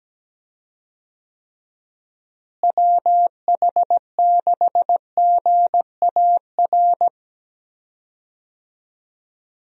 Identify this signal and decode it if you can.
Morse code, heard as a keyed tone, about 17 wpm, 700 Hz. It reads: WH6GAR
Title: USA Call Signs - 6 Characters - Repeated 3x - 17wpm